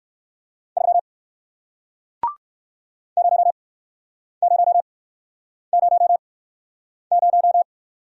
Morse code, heard as a keyed tone, about 62 wpm, spaced as for 9 wpm, 700 Hz. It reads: 3 0000